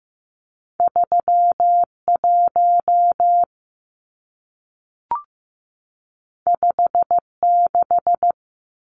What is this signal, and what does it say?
Morse code, heard as a keyed tone, about 15 wpm, 700 Hz. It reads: 31 56
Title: Numbers - 2-Digit 15wpm